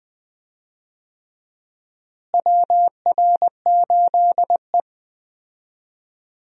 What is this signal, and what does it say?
Morse code, heard as a keyed tone, about 20 wpm, 700 Hz. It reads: WR8E
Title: USA Call Signs - 4 Characters - Repeated 3x - 20wpm